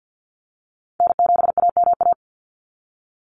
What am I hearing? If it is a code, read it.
Morse code, heard as a keyed tone, about 50 wpm, 700 Hz. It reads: N7AMA
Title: NA SOTA Chaser Callsigns 50wpm